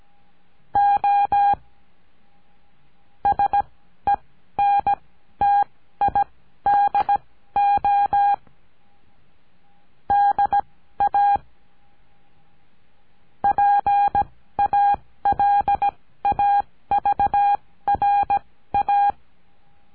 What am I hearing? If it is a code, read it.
Morse code, heard as a keyed tone, about 17 wpm, spaced as for 13 wpm, 810 Hz. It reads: O SENTIDO DA PALAVRA
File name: CURSO DE TELEGRAFIA